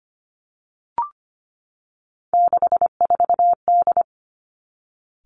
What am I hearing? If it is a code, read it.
Morse code, heard as a keyed tone, about 25 wpm, 700 Hz. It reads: 64D
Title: Three Challenging Letter-Number Combinations 25wpm